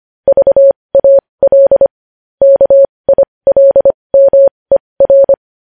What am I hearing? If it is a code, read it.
Morse code, heard as a keyed tone, about 25 wpm, spaced as for 20 wpm, 550 Hz. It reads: VAL KILMER
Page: News Headlines in Morse Code at 20 WPM